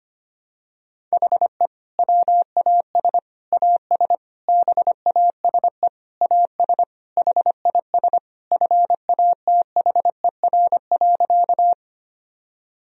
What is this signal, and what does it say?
Morse code, heard as a keyed tone, about 25 wpm, 700 Hz. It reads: HE WAS AS BASE AS HIS FATHER.
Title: Sentences from Top 1000 Words - Review Part 1 - 25wpm